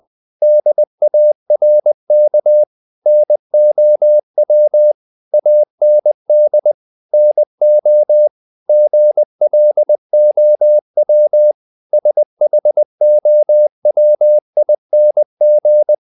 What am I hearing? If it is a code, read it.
Morse code, heard as a keyed tone, about 20 wpm, 600 Hz. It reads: DARK NOW AND NO GLOW SHOWING
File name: the old man and the sea part 2 20wpm